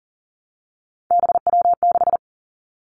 Morse code, heard as a keyed tone, about 38 wpm, 700 Hz. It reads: BW6
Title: Three Challenging Letter-Number Combinations 40wpm